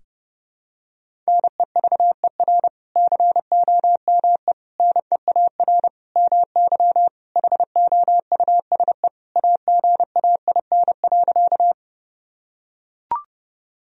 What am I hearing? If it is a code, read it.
Morse code, heard as a keyed tone, about 32 wpm, 700 Hz. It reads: NEVER COME NEAR MY HOUSE AGAIN.
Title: Sentences from Top 1000 Words - Review Part 1 - 30wpm